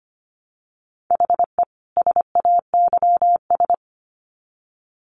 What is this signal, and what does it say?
Morse code, heard as a keyed tone, about 25 wpm, 700 Hz. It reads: HE SAYS
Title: Sets of 2 Words 25wpm